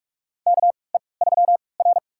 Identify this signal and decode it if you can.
Morse code, heard as a keyed tone, about 41 wpm, spaced as for 25 wpm, 690 Hz. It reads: KE3R
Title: USA Call Signs - 4 Characters - 3x Character Spacing - 45wpm